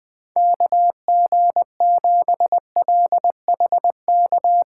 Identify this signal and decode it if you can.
Morse code, heard as a keyed tone, about 21 wpm, 700 Hz. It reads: KG7LHK